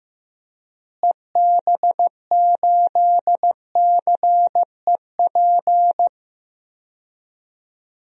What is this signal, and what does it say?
Morse code, heard as a keyed tone, about 15 wpm, 700 Hz. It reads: EB8CEP
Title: DXCC-100 Call Signs - 6 Characters - Repeated 3x - 15wpm